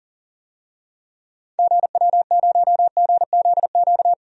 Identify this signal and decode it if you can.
Morse code, heard as a keyed tone, about 40 wpm, 700 Hz. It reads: GW0GZQ
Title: DXCC-100 Call Signs - 6 Characters - 40wpm